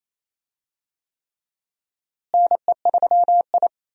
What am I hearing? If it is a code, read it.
Morse code, heard as a keyed tone, about 28 wpm, 700 Hz. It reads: NE3I
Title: USA Call Signs - 4 Characters - Repeated 3x - 28wpm